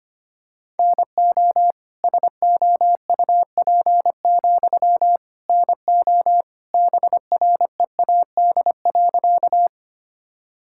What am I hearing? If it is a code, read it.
Morse code, heard as a keyed tone, about 25 wpm, 700 Hz. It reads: NO SOUP, NO BREAD.
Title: Sentences from Top 1600 Words 25wpm